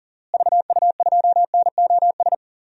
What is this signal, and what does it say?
Morse code, heard as a keyed tone, about 40 wpm, 700 Hz. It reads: VU2NOS
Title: DXCC-100 Call Signs - 6 Characters - Repeated 2x - 40wpm